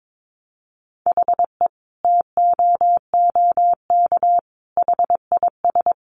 Morse code, heard as a keyed tone, about 22 wpm, 700 Hz. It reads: HE TOOK HIS
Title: Sentences from Top 900 Words 22wpm